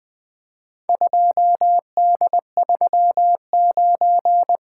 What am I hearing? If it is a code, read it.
Morse code, heard as a keyed tone, about 20 wpm, 700 Hz. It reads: 2D39
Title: Four Letter-Number Combinations - 20wpm